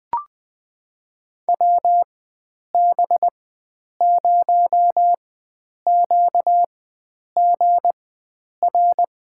Morse code, heard as a keyed tone, about 20 wpm, spaced as for 9 wpm, 700 Hz. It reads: WB0QGR
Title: USA Call Signs - 6 Characters - 4x Character Spacing - 20wpm